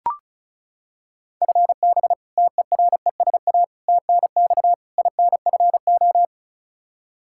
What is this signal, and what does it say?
Morse code, heard as a keyed tone, about 36 wpm, 700 Hz. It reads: FB TERESA TNX INFO
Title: Practice QSO - Exchange 3 - More Info - 35wpm